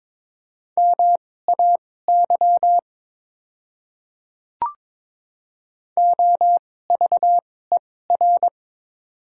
Morse code, heard as a keyed tone, about 23 wpm, spaced as for 16 wpm, 700 Hz. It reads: MAY OVER
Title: 2x Character Spacing - Top 100 Words - 22wpm